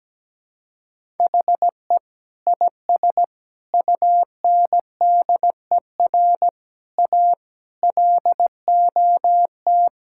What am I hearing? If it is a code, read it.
Morse code, heard as a keyed tone, about 17 wpm, 700 Hz. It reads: HE IS UNDER A LOT
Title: Sentences from Top 1000 Words - Review Part 6 - 17wpm